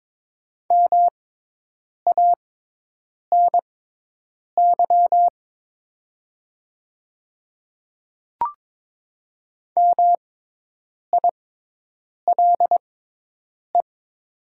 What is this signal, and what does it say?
Morse code, heard as a keyed tone, about 22 wpm, spaced as for 8 wpm, 700 Hz. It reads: MANY MILE
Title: 6x Character Spacing - Top 300 Words - 22wpm